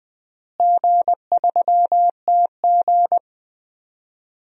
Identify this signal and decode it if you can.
Morse code, heard as a keyed tone, about 21 wpm, 700 Hz. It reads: G3TG
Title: DXCC-100 Call Signs - 4 Characters - Repeated 2x - 20wpm